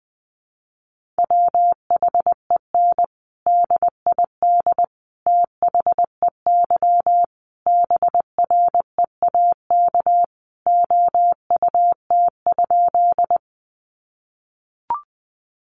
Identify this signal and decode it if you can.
Morse code, heard as a keyed tone, about 19 wpm, 700 Hz. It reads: WHEN DID THEY BREAK OUT?
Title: Sentences from Top 900 Words 20wpm